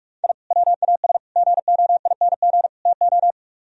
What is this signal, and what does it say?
Morse code, heard as a keyed tone, about 48 wpm, 700 Hz. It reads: I WAS GOING TO